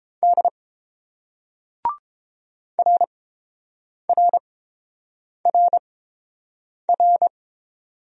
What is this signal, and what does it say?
Morse code, heard as a keyed tone, about 29 wpm, spaced as for 7 wpm, 690 Hz. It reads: D RRRR